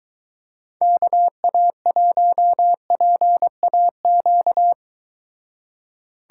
Morse code, heard as a keyed tone, about 24 wpm, 700 Hz. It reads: KA1PAQ